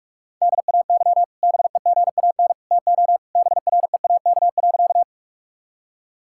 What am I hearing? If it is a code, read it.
Morse code, heard as a keyed tone, about 46 wpm, 700 Hz. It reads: DAY BEGAN TO BREAK.
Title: Sentences from Top 1000 Words - Review Part 6 - 45wpm